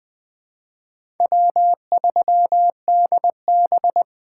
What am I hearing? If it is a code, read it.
Morse code, heard as a keyed tone, about 20 wpm, 700 Hz. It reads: W3DB
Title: USA Call Signs - 3x Character Spacing - 20wpm